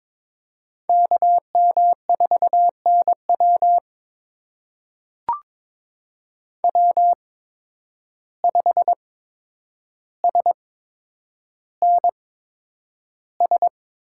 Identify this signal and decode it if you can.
Morse code, heard as a keyed tone, about 21 wpm, spaced as for 9 wpm, 700 Hz. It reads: KM4NW W5SNS
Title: USA Call Signs - 5 Characters - 8x Character Spacing - 22wpm